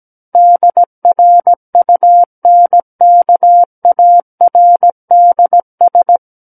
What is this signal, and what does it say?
Morse code, heard as a keyed tone, about 17 wpm, 700 Hz. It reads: DRUNKARDS
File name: Book One / 08-Chapter 8 - 17 WPM